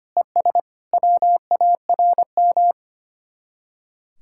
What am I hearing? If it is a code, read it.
Morse code, heard as a keyed tone, about 31 wpm, 700 Hz. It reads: ES WARM
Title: Speed-Racing Common QSO Phrases - 25wpm